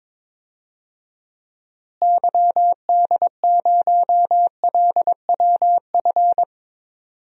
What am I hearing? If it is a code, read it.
Morse code, heard as a keyed tone, about 22 wpm, 700 Hz. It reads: YD0LWF
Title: DXCC-100 Call Signs - 6 Characters - Repeated 3x - 22wpm